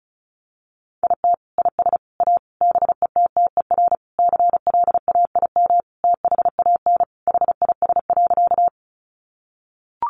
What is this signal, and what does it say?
Morse code, heard as a keyed tone, about 34 wpm, 700 Hz. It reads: IT IS A BETTER CLAIM THAN HIS.
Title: Sentences from Top 1500 Words 35wpm